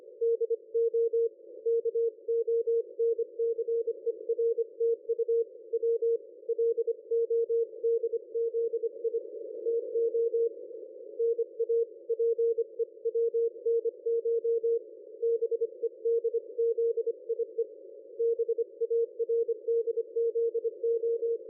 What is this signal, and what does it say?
Morse code, heard as a keyed tone, about 25 wpm, 460 Hz. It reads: DO KONCERTU W LODZI TO NA PEWNCH BEDZIE BARDZO
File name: dlazaawansowanych / nagraniazpasma / ExpertSDR2 line out 23.01.2020 14.04.01